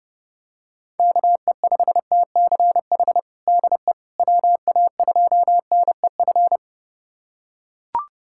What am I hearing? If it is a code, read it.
Morse code, heard as a keyed tone, about 30 wpm, 700 Hz. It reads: KE5TCH DE WA2NEF
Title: USA Callsign DE Callsign 30wpm